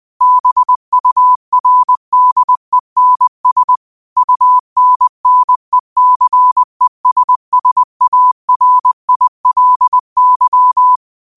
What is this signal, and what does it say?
Morse code, heard as a keyed tone, about 20 wpm, 1000 Hz. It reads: BURDENS UNNECESSARILY
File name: qotdfiles / 20wpm